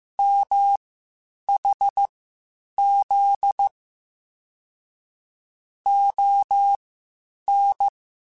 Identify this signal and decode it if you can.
Morse code, heard as a keyed tone, about 15 wpm, spaced as for 8 wpm, 800 Hz. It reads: MHZ ON